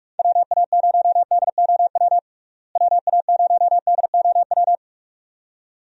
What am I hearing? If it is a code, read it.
Morse code, heard as a keyed tone, about 46 wpm, 700 Hz. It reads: WA0DOW WA0DOW
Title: USA Call Signs - 6 Characters - Repeated 2x - 45wpm